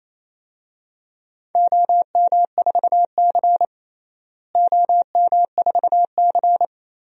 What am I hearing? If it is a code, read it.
Morse code, heard as a keyed tone, about 28 wpm, 700 Hz. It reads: OM4C OM4C
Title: DXCC-100 Call Signs - 4 Characters - Repeated 3x - 28wpm